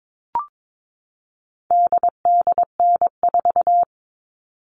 Morse code, heard as a keyed tone, about 22 wpm, 700 Hz. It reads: DDN4
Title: All Cut Numbers - 4-Digit 22wpm